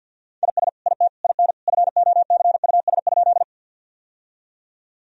Morse code, heard as a keyed tone, about 48 wpm, 700 Hz. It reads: IS IT IN FOCUS?
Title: Sentences from Top 1000 Words - Review Part 4 - 50wpm